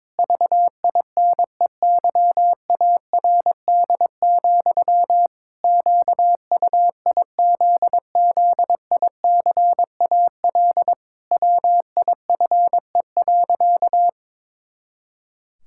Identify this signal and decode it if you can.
Morse code, heard as a keyed tone, about 22 wpm, 700 Hz. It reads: VINEYARD, QUIZZICAL WIFE.
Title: Pangram Challenge v2 - 22wpm